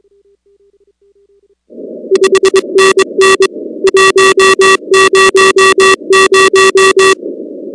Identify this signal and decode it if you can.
Morse code, heard as a keyed tone, about 22 wpm, 380 Hz. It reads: 5NN 100
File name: contests / 2019-cw-wpx-kt5j-k5pi- -2330 0-19